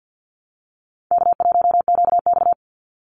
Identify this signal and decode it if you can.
Morse code, heard as a keyed tone, about 50 wpm, 700 Hz. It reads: K1QX
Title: Easy Call Signs - 50wpm